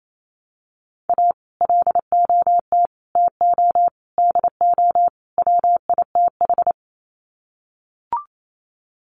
Morse code, heard as a keyed tone, about 28 wpm, 700 Hz. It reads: A LOT TO DO WITH
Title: Sets of 5 Words 28wpm